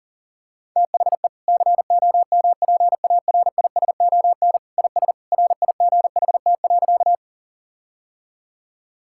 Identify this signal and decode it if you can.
Morse code, heard as a keyed tone, about 40 wpm, 700 Hz. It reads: THE COMPARISON IS RIGHT.
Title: Sentences from Top 1600 Words 40wpm